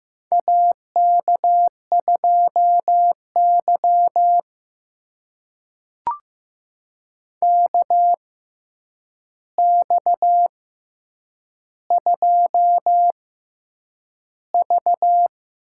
Morse code, heard as a keyed tone, about 15 wpm, spaced as for 8 wpm, 700 Hz. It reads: AK2Y KX2V
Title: USA Call Signs - 4 Characters - 6x Character Spacing - 15wpm